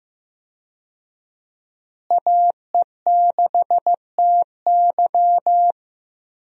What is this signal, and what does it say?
Morse code, heard as a keyed tone, about 15 wpm, 700 Hz. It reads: AE6TY
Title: USA Call Signs - 5 Characters - Repeated 2x - 15wpm